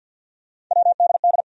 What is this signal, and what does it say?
Morse code, heard as a keyed tone, about 53 wpm, 700 Hz. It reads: WDD
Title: USA Call Sign Postfixes - 3 Letters 50wpm